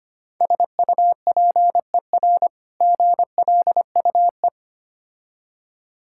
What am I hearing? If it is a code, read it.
Morse code, heard as a keyed tone, about 25 wpm, 700 Hz. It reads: SUPER GLUE
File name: practice / 25wpm / Difficult Words - Speed-Racing with Hints - 25wpm